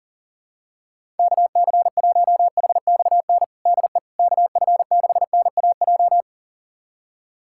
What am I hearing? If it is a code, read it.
Morse code, heard as a keyed tone, about 40 wpm, 700 Hz. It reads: KC1HXN DE KF6NAJ
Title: USA Callsign DE Callsign - 6 Characters - 40wpm